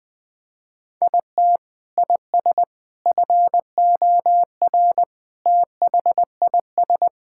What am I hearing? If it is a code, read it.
Morse code, heard as a keyed tone, about 21 wpm, 700 Hz. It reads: IT IS FOR THIS
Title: Sentences with 6-Word Phrases - 20wpm